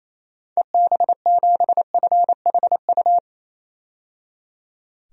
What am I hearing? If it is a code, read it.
Morse code, heard as a keyed tone, about 28 wpm, 700 Hz. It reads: EB7FHU
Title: DXCC-100 Call Signs - 6 Characters - 28wpm